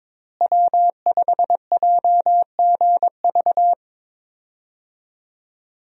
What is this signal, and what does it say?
Morse code, heard as a keyed tone, about 22 wpm, 700 Hz. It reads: W5JGV
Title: USA Call Signs - 5 Characters - Repeated 2x - 22wpm